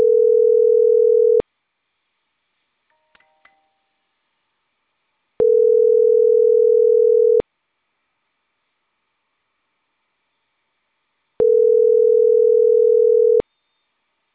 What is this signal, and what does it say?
Ringback tone (North America), 440 + 480 Hz.